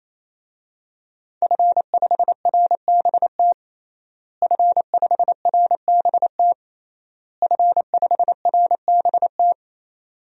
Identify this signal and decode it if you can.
Morse code, heard as a keyed tone, about 28 wpm, 690 Hz. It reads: F5RBT F5RBT F5RBT
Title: DXCC-100 Call Signs - 5 Characters - Repeated 3x - 28wpm